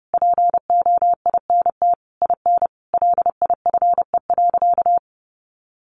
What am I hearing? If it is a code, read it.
Morse code, heard as a keyed tone, about 30 wpm, 700 Hz. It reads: POINT IN LIFE.
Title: Sentences from Top 1200 Words 30wpm